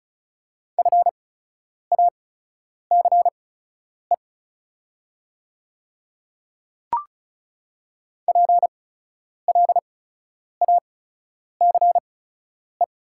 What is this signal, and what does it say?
Morse code, heard as a keyed tone, about 36 wpm, spaced as for 10 wpm, 700 Hz. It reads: FACE PLACE
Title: 8x Character Spacing - Top 500 Words - 35wpm